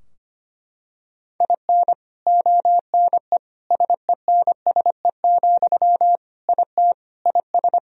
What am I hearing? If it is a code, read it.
Morse code, heard as a keyed tone, about 25 wpm, 700 Hz. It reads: IN ONE SENSE, IT IS
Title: Sentences from Top 1000 Words - Review Part 4 - 25wpm